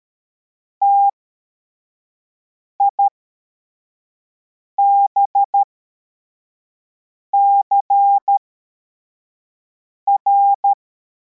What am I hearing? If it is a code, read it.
Morse code, heard as a keyed tone, about 13 wpm, spaced as for 4 wpm, 800 Hz. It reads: TIBCR
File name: formation / cw / Text62